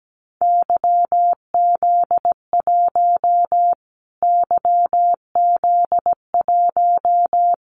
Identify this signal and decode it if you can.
Morse code, heard as a keyed tone, about 17 wpm, 700 Hz. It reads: YZ1 YZ1
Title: Callsign Prefixes for Top 25 DXCC - Repeated 3x - 17wpm